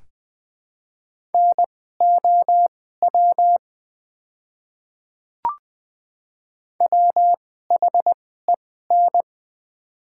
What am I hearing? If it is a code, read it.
Morse code, heard as a keyed tone, about 20 wpm, spaced as for 14 wpm, 700 Hz. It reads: NOW WHEN
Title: 2x Character Spacing - Top 300 Words - 20wpm